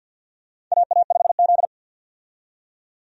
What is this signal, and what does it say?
Morse code, heard as a keyed tone, about 49 wpm, 700 Hz. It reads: AA5Z
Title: USA Call Signs - 4 Characters - 50wpm